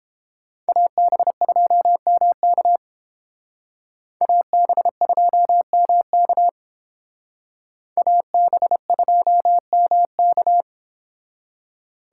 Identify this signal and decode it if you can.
Morse code, heard as a keyed tone, about 29 wpm, 700 Hz. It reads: AB2MK AB2MK AB2MK